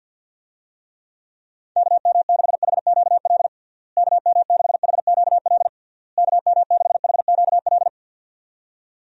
Morse code, heard as a keyed tone, about 46 wpm, 700 Hz. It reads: KM6HQL KM6HQL KM6HQL
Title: USA Call Signs - 6 Characters - Repeated 3x - 50wpm